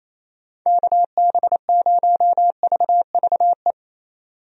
Morse code, heard as a keyed tone, about 28 wpm, 700 Hz. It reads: KB0VVE
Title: USA Call Signs - 6 Characters - 28wpm